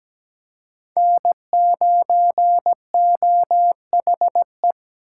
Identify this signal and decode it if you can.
Morse code, heard as a keyed tone, about 20 wpm, 700 Hz. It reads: N9OHE
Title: Speed Racing - USA 1x3 Call Signs 17wpm